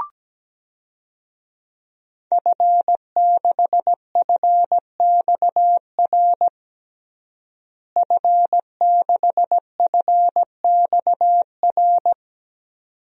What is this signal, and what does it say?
Morse code, heard as a keyed tone, about 17 wpm, 700 Hz. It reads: F6FXR F6FXR
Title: DXCC-100 Call Signs - 5 Characters - Repeated 2x - 17wpm